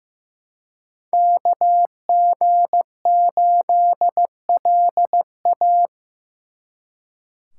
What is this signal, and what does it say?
Morse code, heard as a keyed tone, about 18 wpm, 700 Hz. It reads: KG8LA
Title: Speed Racing - USA Call Signs 15wpm